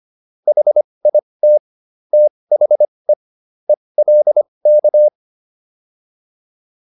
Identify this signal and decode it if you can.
Morse code, heard as a keyed tone, about 25 wpm, spaced as for 20 wpm, 600 Hz. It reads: HIT THE ELK